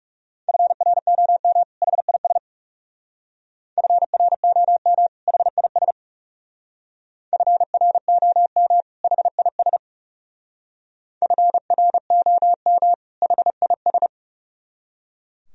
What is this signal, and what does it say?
Morse code, heard as a keyed tone, about 35 wpm, 700 Hz. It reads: FROM HIS FROM HIS FROM HIS FROM HIS
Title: Speed-Racing - Sets of 2 Words 30wpm